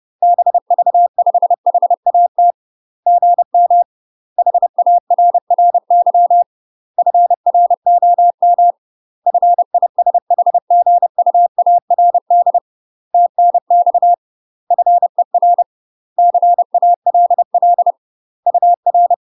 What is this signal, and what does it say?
Morse code, heard as a keyed tone, about 30 wpm, 700 Hz. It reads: BV5HAT GM HARRY FROM FISHGUARD TNX FER CALL UR
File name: UK MASTER QSOs TRAINING / 30WPM / UK MASTER QSOs 0050